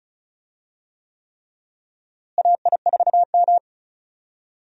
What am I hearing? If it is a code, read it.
Morse code, heard as a keyed tone, about 36 wpm, 700 Hz. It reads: AI4M
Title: USA Call Signs - 4 Characters - Repeated 3x - 35wpm